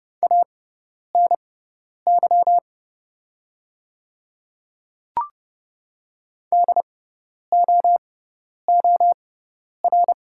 Morse code, heard as a keyed tone, about 30 wpm, spaced as for 10 wpm, 700 Hz. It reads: ANY DOOR
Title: 6x Character Spacing - Top 300 Words - 30wpm